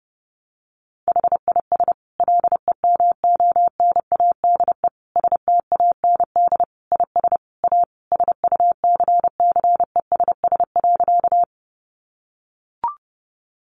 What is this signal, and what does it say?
Morse code, heard as a keyed tone, about 30 wpm, 700 Hz. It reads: HIS LEMONADE STAND IS A SUCCESS.
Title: Additional Easy Phrases 30wpm